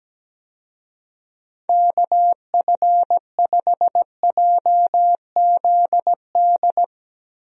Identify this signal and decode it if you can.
Morse code, heard as a keyed tone, about 17 wpm, 700 Hz. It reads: KF5JZD
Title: USA Call Signs - 6 Characters - Repeated 2x - 17wpm